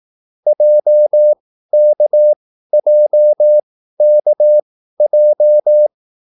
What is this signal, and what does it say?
Morse code, heard as a keyed tone, about 18 wpm, spaced as for 13 wpm, 600 Hz. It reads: JKJKJ
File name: QSL / CW lesson